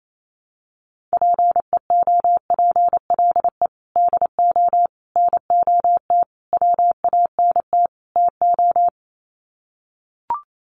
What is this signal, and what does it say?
Morse code, heard as a keyed tone, about 28 wpm, 700 Hz. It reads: PEOPLE DO NOT WANT TO
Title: Sets of 5 Words Encore2 28wpm